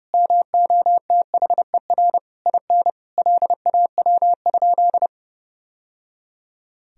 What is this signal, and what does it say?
Morse code, heard as a keyed tone, about 28 wpm, 700 Hz. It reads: MOTHER IN LAW?
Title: Sentences from Top 1000 Words - Review Part 4 - 30wpm